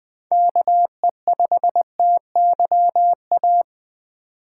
Morse code, heard as a keyed tone, about 20 wpm, 700 Hz. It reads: KE5TYA
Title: USA Call Signs - 6 Characters - 4x Character Spacing - 20wpm